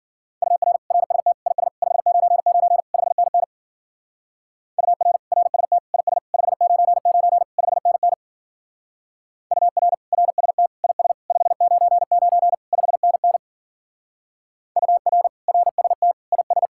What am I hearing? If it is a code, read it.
Morse code, heard as a keyed tone, about 52 wpm, 700 Hz. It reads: UR RST IS 599 5NN UR RST IS 599 5NN UR RST IS 599 5NN UR RST IS